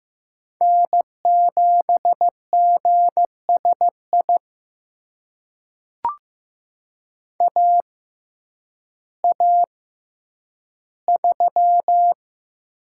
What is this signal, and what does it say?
Morse code, heard as a keyed tone, about 15 wpm, spaced as for 9 wpm, 700 Hz. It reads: N7GSI AA3
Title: USA Call Signs - 5 Characters - 6x Character Spacing - 15wpm